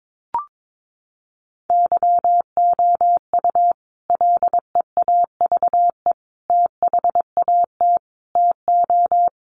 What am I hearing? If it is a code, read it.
Morse code, heard as a keyed tone, about 22 wpm, 700 Hz. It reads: YOU LEAVE THAT TO
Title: Sentences from Top 1400 Words 22wpm